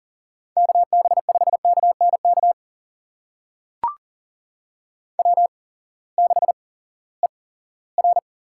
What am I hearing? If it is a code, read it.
Morse code, heard as a keyed tone, about 42 wpm, spaced as for 20 wpm, 700 Hz. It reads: KB5KNK W6ER
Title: USA Call Signs - 8x Character Spacing - 40wpm